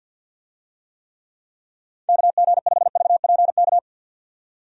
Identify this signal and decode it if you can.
Morse code, heard as a keyed tone, about 52 wpm, 700 Hz. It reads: KG5VPK
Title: USA Call Signs - 6 Characters - 50wpm